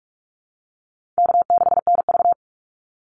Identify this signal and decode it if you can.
Morse code, heard as a keyed tone, about 44 wpm, 700 Hz. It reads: K6NV
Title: Easy Call Signs - 45wpm